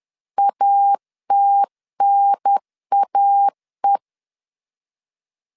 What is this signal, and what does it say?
Morse code, heard as a keyed tone, about 10 wpm, 800 Hz. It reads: ATNAE
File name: CW / DET / 1 3 1500C S C 10wpm 3335